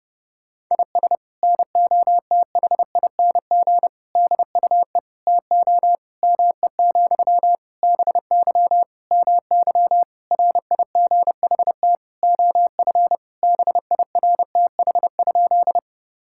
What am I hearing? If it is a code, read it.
Morse code, heard as a keyed tone, about 30 wpm, 700 Hz. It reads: IS NOTHING DUE TO ME, BY MY RIGHT OF BIRTH?
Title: Sentences from Top 1000 Words - Review Part 6 - 30wpm